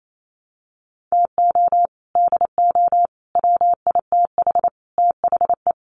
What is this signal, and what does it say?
Morse code, heard as a keyed tone, about 28 wpm, 700 Hz. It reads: TO DO WITH THE
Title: Sets of 4 Words 28wpm